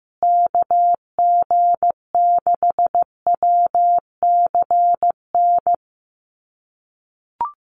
Morse code, heard as a keyed tone, about 15 wpm, 700 Hz. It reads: KG6WCN
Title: USA Call Signs Repeated 3x 15wpm